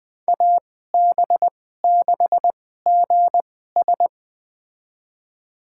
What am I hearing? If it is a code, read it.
Morse code, heard as a keyed tone, about 20 wpm, spaced as for 15 wpm, 700 Hz. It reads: AB6GS
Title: USA Call Signs - 5 Characters - 2x Character Spacing - 20wpm